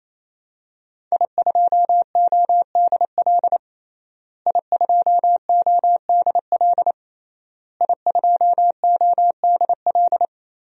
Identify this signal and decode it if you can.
Morse code, heard as a keyed tone, about 27 wpm, 700 Hz. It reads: I2ODL I2ODL I2ODL
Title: DXCC-100 Call Signs - 5 Characters - Repeated 3x - 28wpm